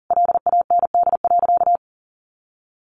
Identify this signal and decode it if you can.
Morse code, heard as a keyed tone, about 40 wpm, 700 Hz. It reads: LAND.
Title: Sentences from Top 1200 Words 40wpm